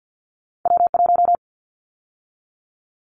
Morse code, heard as a keyed tone, about 50 wpm, 700 Hz. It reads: W1
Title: Callsign Prefixes for Top 25 DXCC - 50wpm